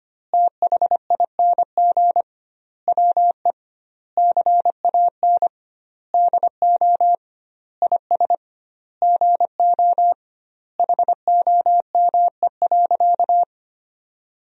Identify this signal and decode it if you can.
Morse code, heard as a keyed tone, about 25 wpm, 700 Hz. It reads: THING WE CAN DO IS GO HOME.
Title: 2x Word-Spacing - Sentences from Top 200 Words - 25wpm